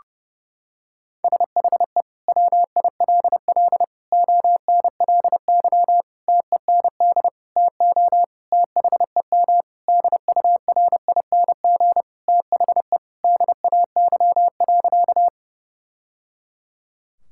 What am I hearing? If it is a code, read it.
Morse code, heard as a keyed tone, about 32 wpm, 700 Hz. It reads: SHE WILL ONLY TEND TO THEM DURING THE DAY.
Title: Sentences from Top 1000 Words - Review Part 5 - 30wpm